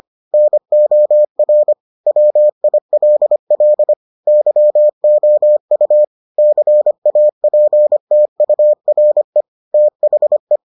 Morse code, heard as a keyed tone, about 25 wpm, 600 Hz. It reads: NOR WILL YOU CAPTURE THE